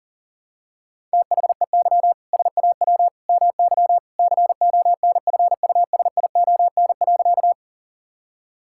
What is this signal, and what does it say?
Morse code, heard as a keyed tone, about 37 wpm, 700 Hz. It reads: THEY SAW MY CONFUSION.
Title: Sentences from Top 1600 Words 40wpm